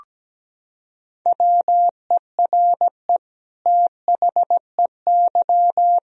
Morse code, heard as a keyed tone, about 17 wpm, 700 Hz. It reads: WERE THEY
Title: Sentences from Top 1000 Words - Review Part 3 - 17wpm